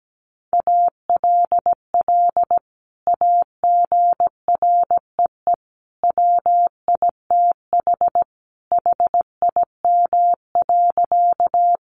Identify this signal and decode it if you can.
Morse code, heard as a keyed tone, about 17 wpm, 700 Hz. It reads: ALL AGREE WITH HIM.
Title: Sentences from Top 1100 Words 17wpm